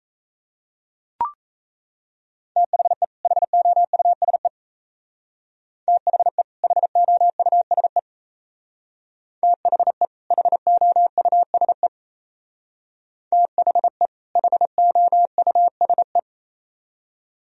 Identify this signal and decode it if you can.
Morse code, heard as a keyed tone, about 34 wpm, 700 Hz. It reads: THE HOUSE THE HOUSE THE HOUSE THE HOUSE
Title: Speed-Racing - Sets of 2 Words 28wpm